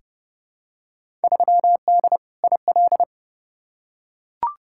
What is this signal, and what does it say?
Morse code, heard as a keyed tone, about 30 wpm, 700 Hz. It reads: 3D IL
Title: Field Day Exchanges Encore - 30wpm